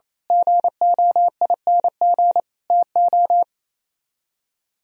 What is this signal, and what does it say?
Morse code, heard as a keyed tone, about 33 wpm, 700 Hz. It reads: GOING TO
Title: Speed-Racing - Sets of 5 Words 28wpm